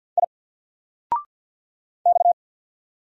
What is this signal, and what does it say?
Morse code, heard as a keyed tone, about 48 wpm, 700 Hz. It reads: I X
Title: Single Letter-Number Alt 50wpm